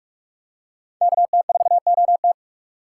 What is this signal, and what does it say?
Morse code, heard as a keyed tone, about 46 wpm, 700 Hz. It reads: KT4OT
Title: USA Call Signs - 5 Characters - 3x Character Spacing - 45wpm